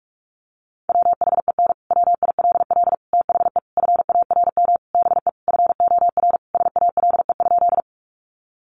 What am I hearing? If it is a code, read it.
Morse code, heard as a keyed tone, about 44 wpm, 700 Hz. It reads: WHEN WILL THE FARM BE FOR SALE?
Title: Sentences from Top 1400 Words 45wpm